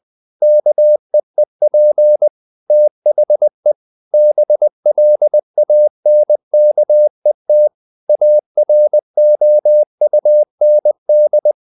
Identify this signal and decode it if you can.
Morse code, heard as a keyed tone, about 20 wpm, 600 Hz. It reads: KEEP THE BLANKET AROUND
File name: the old man and the sea / the old man and the sea part 1 20wpm